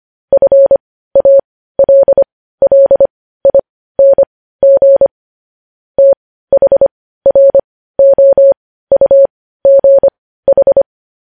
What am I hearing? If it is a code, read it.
Morse code, heard as a keyed tone, about 25 wpm, spaced as for 15 wpm, 550 Hz. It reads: FALLING THROUGH
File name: podcast / News headlines-1x-25x15